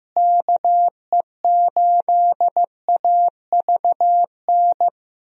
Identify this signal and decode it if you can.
Morse code, heard as a keyed tone, about 15 wpm, 700 Hz. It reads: KE8AVN
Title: USA Call Signs - 6 Characters - Repeated 3x - 15wpm